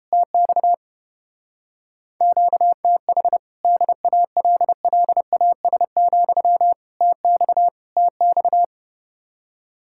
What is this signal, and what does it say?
Morse code, heard as a keyed tone, about 29 wpm, 700 Hz. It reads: TX QTH DALLAS, TX TX